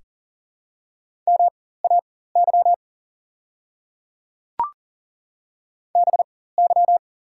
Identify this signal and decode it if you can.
Morse code, heard as a keyed tone, about 40 wpm, spaced as for 19 wpm, 690 Hz. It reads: MAY BY
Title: 4x Character Spacing - Top 200 Words - 40wpm